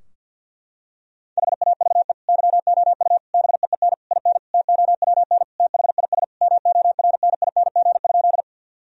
Morse code, heard as a keyed tone, about 52 wpm, 700 Hz. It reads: HAVE YOU BEEN IN TOWN THIS MORNING?
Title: Sentences from Top 1000 Words - Review Part 2 - 50wpm